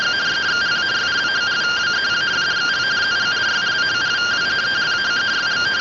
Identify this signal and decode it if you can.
Two-tone teleprinter signal, about 170 Hz shift.